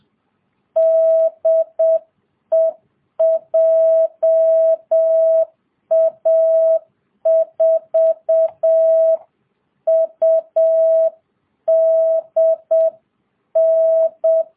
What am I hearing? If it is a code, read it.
Morse code, heard as a keyed tone, about 7 wpm, 660 Hz. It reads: DE JA4UDN